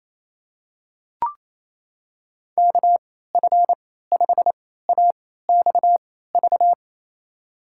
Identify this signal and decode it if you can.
Morse code, heard as a keyed tone, about 28 wpm, spaced as for 16 wpm, 700 Hz. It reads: KF5AXV
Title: USA Call Signs - 6 Characters - 3x Character Spacing - 28wpm